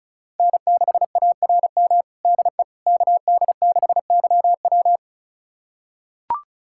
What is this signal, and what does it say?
Morse code, heard as a keyed tone, about 36 wpm, 700 Hz. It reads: N6ARM DE KD6YW
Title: USA Callsign DE Callsign - 5 Characters - 35wpm